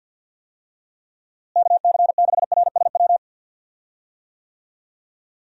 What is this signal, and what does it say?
Morse code, heard as a keyed tone, about 52 wpm, 700 Hz. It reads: KC6RSW
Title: USA Call Signs - 6 Characters - Repeated 3x - 50wpm